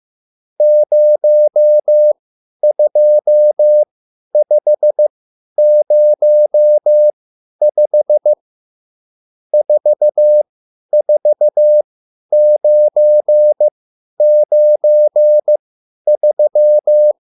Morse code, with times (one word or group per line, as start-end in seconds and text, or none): 0.6-8.3 02505
9.5-17.1 44993